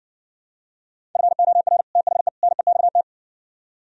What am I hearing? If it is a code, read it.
Morse code, heard as a keyed tone, about 59 wpm, 700 Hz. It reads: FOR THE NEXT